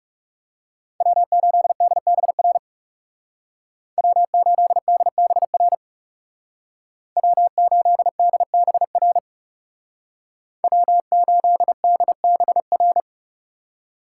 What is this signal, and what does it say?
Morse code, heard as a keyed tone, about 35 wpm, 700 Hz. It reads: W8DBR W8DBR W8DBR W8DBR
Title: Speed Racing - USA 1x3 Call Signs 30wpm